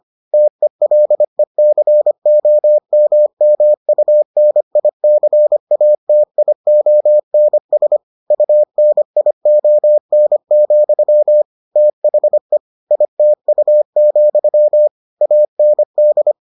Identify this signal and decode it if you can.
Morse code, heard as a keyed tone, about 25 wpm, 600 Hz. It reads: TELECOMMUNICATIONS UNION, THE ITU, AND